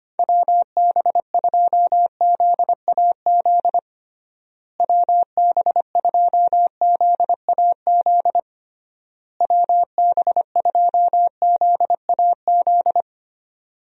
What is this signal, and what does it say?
Morse code, heard as a keyed tone, about 25 wpm, 700 Hz. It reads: WB2ZAZ WB2ZAZ WB2ZAZ
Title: USA Call Signs - 6 Characters - Repeated 3x - 25wpm